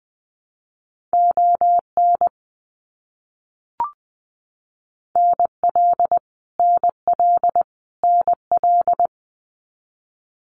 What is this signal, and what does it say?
Morse code, heard as a keyed tone, about 20 wpm, 700 Hz. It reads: ON NL NL NL
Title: Canadian Provinces and Territories - Repeated 3x - 20wpm